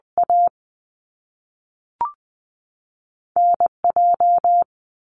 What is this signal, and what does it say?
Morse code, heard as a keyed tone, about 20 wpm, spaced as for 5 wpm, 700 Hz. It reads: ANJ
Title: US States 20wpm